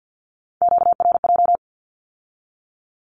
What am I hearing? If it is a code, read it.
Morse code, heard as a keyed tone, about 50 wpm, 700 Hz. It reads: QRJ
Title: Q-Codes with Spoken Meaning - 50wpm